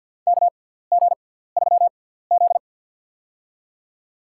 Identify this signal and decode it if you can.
Morse code, heard as a keyed tone, about 53 wpm, spaced as for 17 wpm, 700 Hz. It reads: KG3Z
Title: USA Call Signs - 6x Character Spacing - 50wpm